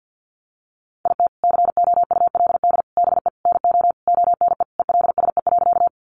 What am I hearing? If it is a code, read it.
Morse code, heard as a keyed tone, about 50 wpm, 700 Hz. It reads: IT COULD BE NO ONE ELSE.
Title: Sentences from Top 800 Words 50wpm